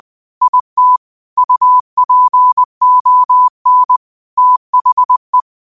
Morse code, heard as a keyed tone, about 20 wpm, 1000 Hz. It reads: IT UPON THE
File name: qotdfiles / 20wpm